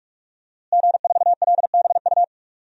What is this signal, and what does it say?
Morse code, heard as a keyed tone, about 46 wpm, 700 Hz. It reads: G4LBU
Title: DXCC-100 Call Signs - 5 Characters - Repeated 3x - 45wpm